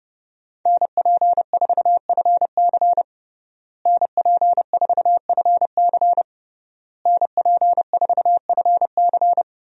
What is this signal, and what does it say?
Morse code, heard as a keyed tone, about 32 wpm, 700 Hz. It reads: NP4FC NP4FC NP4FC
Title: USA Call Signs - 5 Characters - Repeated 3x - 30wpm